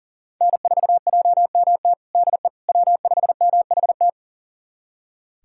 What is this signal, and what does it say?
Morse code, heard as a keyed tone, about 37 wpm, 700 Hz. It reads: N4JMT DE W5MHT
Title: USA Callsign DE Callsign - 5 Characters - 40wpm